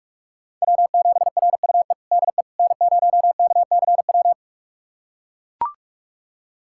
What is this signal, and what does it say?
Morse code, heard as a keyed tone, about 43 wpm, 700 Hz. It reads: W7RUE DE N0KCW
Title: USA Callsign DE Callsign - 5 Characters - 45wpm